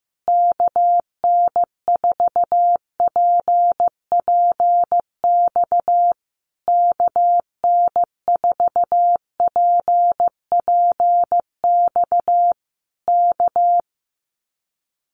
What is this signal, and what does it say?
Morse code, heard as a keyed tone, about 15 wpm, 700 Hz. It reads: KN4PPX KN4PPX K
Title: Common QSO Phrases Encore 15wpm